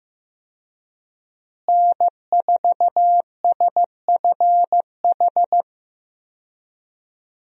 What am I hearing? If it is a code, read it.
Morse code, heard as a keyed tone, about 15 wpm, 700 Hz. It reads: N4SFH
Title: USA Call Signs - 5 Characters - Repeated 2x - 15wpm